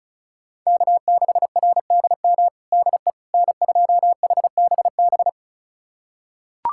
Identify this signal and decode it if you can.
Morse code, heard as a keyed tone, about 36 wpm, 700 Hz. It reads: K6RDM DE N2HBB
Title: USA Callsign DE Callsign - 5 Characters - 35wpm